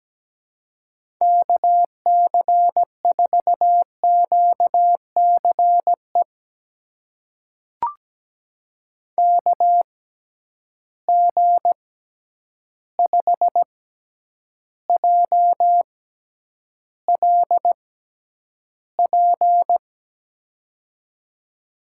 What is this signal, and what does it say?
Morse code, heard as a keyed tone, about 17 wpm, spaced as for 9 wpm, 700 Hz. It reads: KC4QCE KG5JLP
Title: USA Call Signs - 6 Characters - 6x Character Spacing - 17wpm